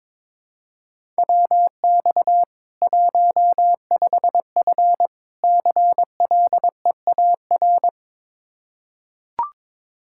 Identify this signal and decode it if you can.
Morse code, heard as a keyed tone, about 22 wpm, 700 Hz. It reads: WX 15F CLEAR
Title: Practice QSO - Exchange 2 - Info - 22wpm